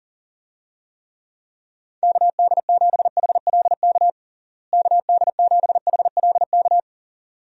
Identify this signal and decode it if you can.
Morse code, heard as a keyed tone, about 40 wpm, 700 Hz. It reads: KD7HLK KD7HLK
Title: USA Call Signs - 6 Characters - Repeated 2x - 40wpm